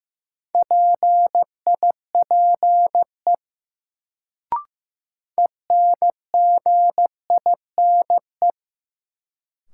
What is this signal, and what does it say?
Morse code, heard as a keyed tone, about 16 wpm, 700 Hz. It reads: PIPE ENGINE
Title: Sentences from Top 1500 Words - New Word Drill Alt - 15wpm